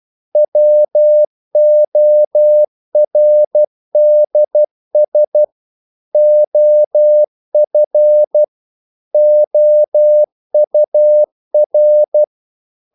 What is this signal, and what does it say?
Morse code, heard as a keyed tone, about 12 wpm, 600 Hz. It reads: WORDS OF OUR
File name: CW / Practice / EnglishQuotations0112